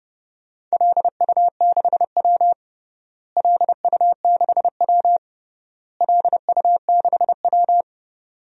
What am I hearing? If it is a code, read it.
Morse code, heard as a keyed tone, about 30 wpm, 700 Hz. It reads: LU6W LU6W LU6W
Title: DXCC-100 Call Signs - 4 Characters - Repeated 3x - 30wpm